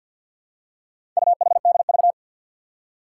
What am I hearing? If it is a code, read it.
Morse code, heard as a keyed tone, about 52 wpm, 700 Hz. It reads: UHDV
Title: Four Challenging Letter-Number Combinations 50wpm